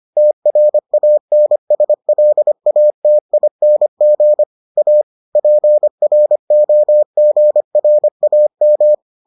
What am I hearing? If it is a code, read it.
Morse code, heard as a keyed tone, about 25 wpm, 600 Hz. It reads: TRANSLATING A PROGRAM